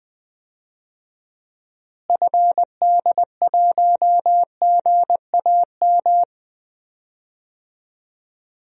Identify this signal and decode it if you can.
Morse code, heard as a keyed tone, about 20 wpm, 690 Hz. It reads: FD1GAM
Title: DXCC-100 Call Signs - 6 Characters - Repeated 2x - 20wpm